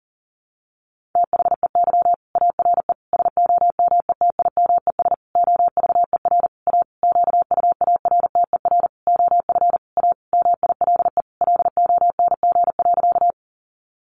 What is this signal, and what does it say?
Morse code, heard as a keyed tone, about 40 wpm, 700 Hz. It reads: THEY ARE SOMETIMES OVER A QUARTER OF A MILE LONG.
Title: Sentences from Top 1300 Words 40wpm